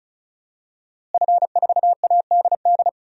Morse code, heard as a keyed tone, about 36 wpm, 700 Hz. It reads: F4ADD
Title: DXCC-100 Call Signs - 5 Characters - Repeated 2x - 35wpm